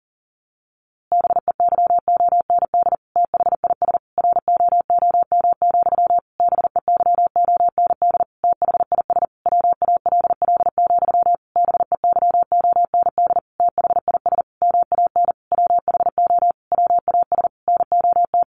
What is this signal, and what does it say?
Morse code, heard as a keyed tone, about 40 wpm, 700 Hz. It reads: BEYOND THIS ROOM, BEYOND THIS WALL, BEYOND THIS MAN WHO WAS NOT
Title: A Sound Of Thunder 40wpm